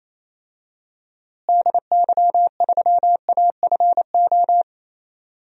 Morse code, heard as a keyed tone, about 28 wpm, 700 Hz. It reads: DY3AFO
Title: DXCC-100 Call Signs - 6 Characters - Repeated 2x - 28wpm